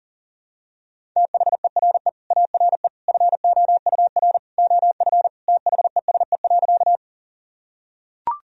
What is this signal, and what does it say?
Morse code, heard as a keyed tone, about 40 wpm, 700 Hz. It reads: THERE ARE FOUR OF THESE.
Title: Sentences from Top 1000 Words - Review Part 1 - 40wpm